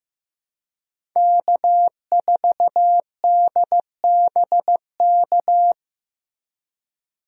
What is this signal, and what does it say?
Morse code, heard as a keyed tone, about 15 wpm, 700 Hz. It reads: K4DBK
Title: USA Call Signs - 5 Characters - Repeated 2x - 15wpm